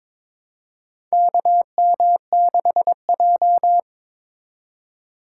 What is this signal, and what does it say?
Morse code, heard as a keyed tone, about 21 wpm, 700 Hz. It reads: KM6J
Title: USA Call Signs - 4 Characters - 4x Character Spacing - 22wpm